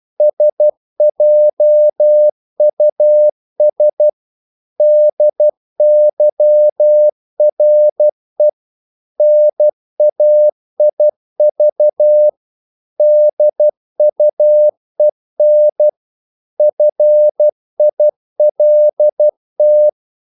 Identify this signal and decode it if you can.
Morse code, heard as a keyed tone, about 12 wpm, 600 Hz. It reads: SJUS DYRE NAIV DUEN FILT
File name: Korte ord DK 12wpm